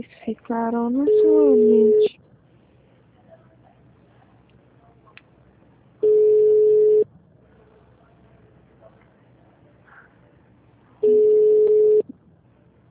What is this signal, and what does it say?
Ringback tone (Europe, 425 Hz), with speech.
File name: out- -2013